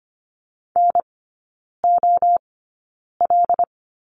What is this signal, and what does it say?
Morse code, heard as a keyed tone, about 25 wpm, spaced as for 9 wpm, 700 Hz. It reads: NOL
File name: course / 25:10wpm / 093 Lesson 16e - Letter c - Three Characters 25wpm